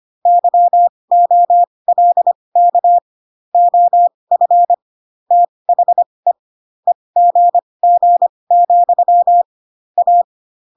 Morse code, heard as a keyed tone, about 25 wpm, spaced as for 20 wpm, 700 Hz. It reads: YOLK OF THE EGG, A